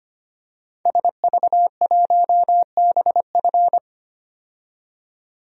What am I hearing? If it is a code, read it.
Morse code, heard as a keyed tone, about 25 wpm, 700 Hz. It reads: SV1BF
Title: DXCC-100 Call Signs - 5 Characters - 25wpm